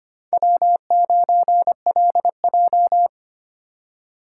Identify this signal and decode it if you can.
Morse code, heard as a keyed tone, about 32 wpm, 700 Hz. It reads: W9LJ